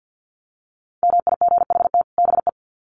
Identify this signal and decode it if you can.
Morse code, heard as a keyed tone, about 50 wpm, 700 Hz. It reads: MIGHT BE
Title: Sets of 2 Words 50wpm